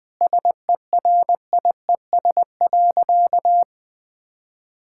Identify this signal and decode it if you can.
Morse code, heard as a keyed tone, about 20 wpm, 700 Hz. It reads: SERIES.
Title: Sentences from Top 1000 Words - Review Part 5 - 20wpm